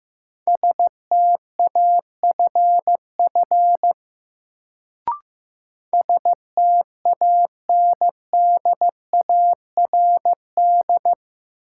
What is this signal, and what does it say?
Morse code, heard as a keyed tone, about 15 wpm, 700 Hz. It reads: STAFF STANDARD
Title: Sentences from Top 1200 Words - New Word Drill Alt - 15wpm